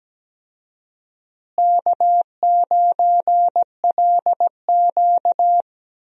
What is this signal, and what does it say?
Morse code, heard as a keyed tone, about 17 wpm, 700 Hz. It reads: K9LQ
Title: USA Call Signs - 4 Characters - Repeated 3x - 17wpm